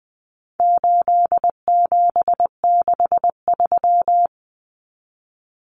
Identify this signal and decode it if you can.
Morse code, heard as a keyed tone, about 20 wpm, 700 Hz. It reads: 8763
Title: Common Cut Numbers - 4-Digit 20wpm